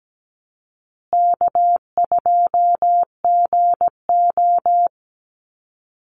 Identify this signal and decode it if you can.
Morse code, heard as a keyed tone, about 17 wpm, 700 Hz. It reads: K2GO
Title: Easy Call Signs Repeated 3x - 17wpm